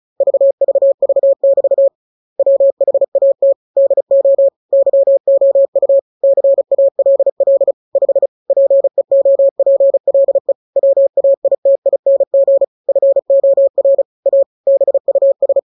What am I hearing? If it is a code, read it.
Morse code, heard as a keyed tone, about 34 wpm, 550 Hz. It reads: VVV<BT> WHAT DO YOU CALL 5 PEOPLE WAITING FOR A BUS
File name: jokes / joke 25 35 0000